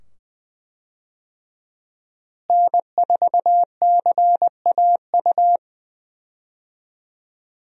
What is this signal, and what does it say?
Morse code, heard as a keyed tone, about 20 wpm, 700 Hz. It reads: N4CAU
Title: USA Call Signs - 5 Characters - Repeated 2x - 20wpm